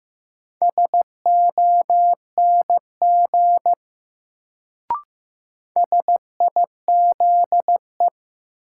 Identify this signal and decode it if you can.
Morse code, heard as a keyed tone, about 15 wpm, 700 Hz. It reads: SONG SIZE
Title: Sentences from Top 400 Words - New Word Drill - 15wpm